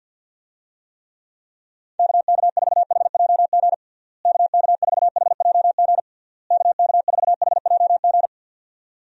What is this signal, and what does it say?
Morse code, heard as a keyed tone, about 52 wpm, 700 Hz. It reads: KK4HJG KK4HJG KK4HJG
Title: USA Call Signs - 6 Characters - Repeated 3x - 50wpm